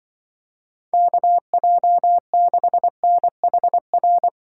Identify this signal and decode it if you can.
Morse code, heard as a keyed tone, about 25 wpm, 700 Hz. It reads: KJ6NHR
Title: Speed Racing - USA 2x3 Call Signs 20wpm